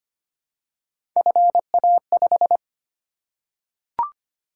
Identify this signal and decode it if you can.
Morse code, heard as a keyed tone, about 25 wpm, 700 Hz. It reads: FA5
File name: Three Letter-Number Combinations - 25wpm